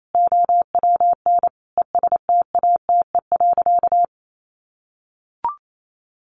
Morse code, heard as a keyed tone, about 28 wpm, 700 Hz. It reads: OWN ESTATE.
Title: Sentences from Top 1200 Words 28wpm